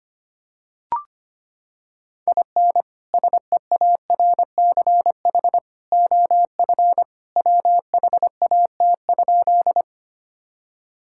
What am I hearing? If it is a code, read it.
Morse code, heard as a keyed tone, about 25 wpm, 700 Hz. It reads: IN SEARCH OF WHAT?
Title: Sentences from Top 1000 Words - Review Part 5 - 25wpm